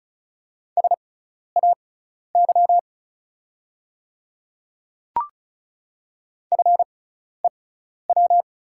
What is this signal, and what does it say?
Morse code, heard as a keyed tone, about 36 wpm, spaced as for 12 wpm, 690 Hz. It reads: SAY FEW
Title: 6x Character Spacing - Top 300 Words - 35wpm